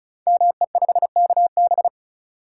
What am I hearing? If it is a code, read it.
Morse code, heard as a keyed tone, about 36 wpm, 700 Hz. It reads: ME5KB
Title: Code Groups - 5 Letter:Numbers - 35wpm